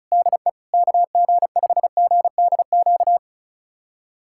Morse code, heard as a keyed tone, about 34 wpm, 690 Hz. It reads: DE KG5GDQ
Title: USA Callsign DE Callsign 35wpm